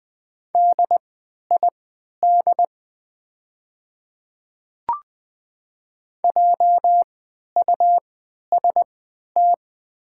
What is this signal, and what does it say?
Morse code, heard as a keyed tone, about 20 wpm, spaced as for 11 wpm, 700 Hz. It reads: DID JUST
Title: 3x Character Spacing - Top 200 Words - 20wpm